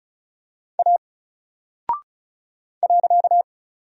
Morse code, heard as a keyed tone, about 34 wpm, 700 Hz. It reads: A .
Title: Single Character (Letter:Number:Punctuation:Prosign) Alt 35wpm